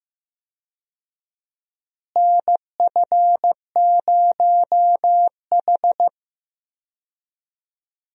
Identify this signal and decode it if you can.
Morse code, heard as a keyed tone, about 15 wpm, 700 Hz. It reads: NF0H
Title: USA Call Signs - 4 Characters - Repeated 2x - 15wpm